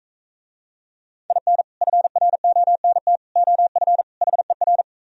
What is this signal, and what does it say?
Morse code, heard as a keyed tone, about 41 wpm, 690 Hz. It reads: IN FRONT OF HER